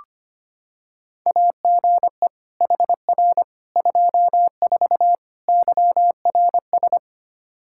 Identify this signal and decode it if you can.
Morse code, heard as a keyed tone, about 25 wpm, 700 Hz. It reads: AGE HR 24 YRS
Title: K1USN SST QSO Practice 25wpm